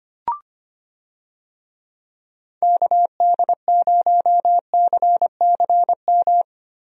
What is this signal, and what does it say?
Morse code, heard as a keyed tone, about 25 wpm, 700 Hz. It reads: KD0CCM
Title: USA Call Signs - 6 Characters - Repeated 2x - 25wpm